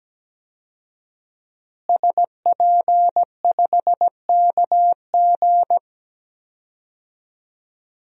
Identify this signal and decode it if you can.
Morse code, heard as a keyed tone, about 17 wpm, 700 Hz. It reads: SP5KG
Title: DXCC-100 Call Signs - 5 Characters - Repeated 2x - 17wpm